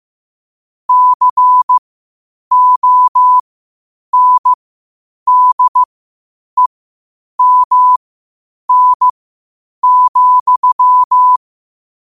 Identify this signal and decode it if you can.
Morse code, heard as a keyed tone, about 15 wpm, spaced as for 8 wpm, 1000 Hz. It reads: CONDEMN,